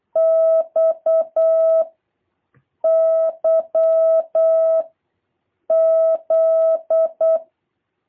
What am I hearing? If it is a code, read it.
Morse code, heard as a keyed tone, about 8 wpm, 650 Hz. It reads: X Y Z